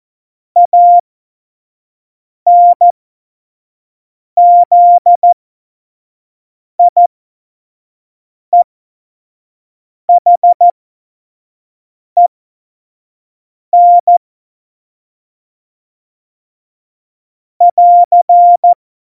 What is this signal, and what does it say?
Morse code, heard as a keyed tone, about 14 wpm, spaced as for 5 wpm, 700 Hz. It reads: ANZIEHEN <AR>
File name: Morsekurs / Witze / Witz 22. Mai 2018, 25 F70